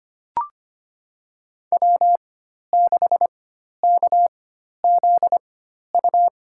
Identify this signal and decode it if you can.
Morse code, heard as a keyed tone, about 25 wpm, spaced as for 12 wpm, 700 Hz. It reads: W6KZU
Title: USA Call Signs - 5 Characters - 4x Character Spacing - 25wpm